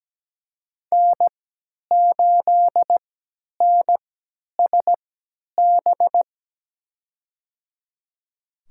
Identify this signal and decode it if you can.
Morse code, heard as a keyed tone, about 17 wpm, spaced as for 10 wpm, 700 Hz. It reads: N8NSB
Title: USA Call Signs - 5 Characters - 3x Character Spacing - 17wpm